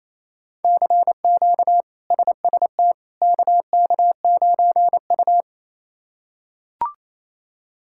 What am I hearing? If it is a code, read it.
Morse code, heard as a keyed tone, about 28 wpm, 700 Hz. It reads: CQ SST KK9U
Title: K1USN SST QSO Practice - 28wpm